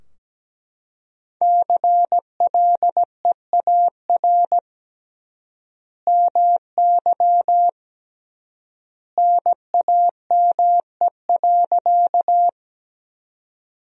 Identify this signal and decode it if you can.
Morse code, heard as a keyed tone, about 17 wpm, 700 Hz. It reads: CLEAR MY NAME.
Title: 3x Word-Spacing - Sentences from Top 500 Words - 17wpm